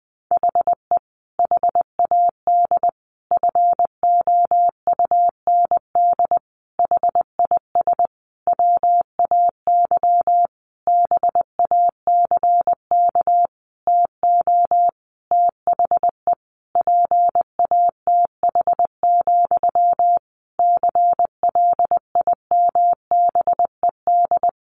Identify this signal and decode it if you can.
Morse code, heard as a keyed tone, about 20 wpm, 700 Hz. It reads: HE HAD FOUND HIS WAY BACK TO THE PATH, CLIMBED
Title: A Sound Of Thunder 20wpm